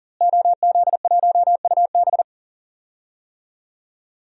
Morse code, heard as a keyed tone, about 40 wpm, 700 Hz. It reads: OZ1UB
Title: DXCC-100 Call Signs - 5 Characters - Repeated 2x - 40wpm